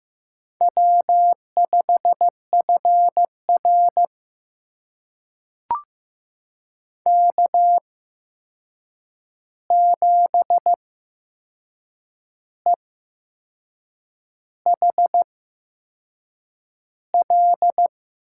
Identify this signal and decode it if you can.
Morse code, heard as a keyed tone, about 15 wpm, spaced as for 6 wpm, 700 Hz. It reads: W5FR K7EHL
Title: USA Call Signs - 8x Character Spacing - 15wpm